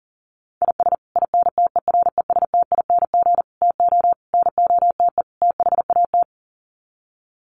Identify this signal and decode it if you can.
Morse code, heard as a keyed tone, about 40 wpm, 700 Hz. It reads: IS INTERESTING TO NOTE THAT
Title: Sets of 5 Words Encore2 40wpm